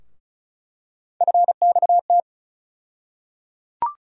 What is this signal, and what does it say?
Morse code, heard as a keyed tone, about 34 wpm, 700 Hz. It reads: FXT
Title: USA Call Sign Postfixes - 3 Letters 35wpm